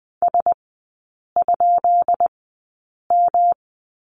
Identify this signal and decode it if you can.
Morse code, heard as a keyed tone, about 20 wpm, spaced as for 8 wpm, 700 Hz. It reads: S?M
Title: 133 Lesson 22e - Question Mark - Three Characters 20wpm